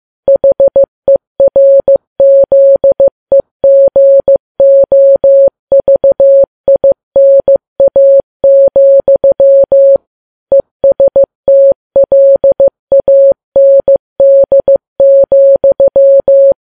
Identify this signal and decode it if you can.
Morse code, heard as a keyed tone, about 15 wpm, 550 Hz. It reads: HERZEGOVINA, ESTLAND,
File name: Files Web tls / Morse on9fv / morse15kps